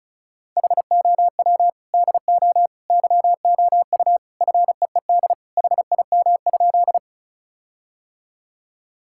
Morse code, heard as a keyed tone, about 36 wpm, 700 Hz. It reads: HOW DO YOU FEED HIM?
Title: Sentences from Top 1600 Words 35wpm